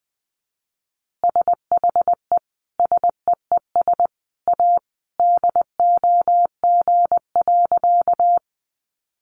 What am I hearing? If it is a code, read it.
Morse code, heard as a keyed tone, about 20 wpm, 700 Hz. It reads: SHE SEES A DOG.
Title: Easy Phrases 20wpm